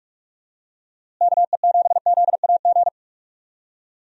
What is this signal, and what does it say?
Morse code, heard as a keyed tone, about 46 wpm, 700 Hz. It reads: KE7ZAG
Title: USA Call Signs - 6 Characters - Repeated 3x - 45wpm